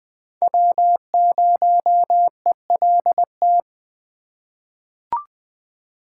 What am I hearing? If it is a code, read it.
Morse code, heard as a keyed tone, about 20 wpm, 700 Hz. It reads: W0ELT
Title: USA Call Signs - 4x Character Spacing - 20wpm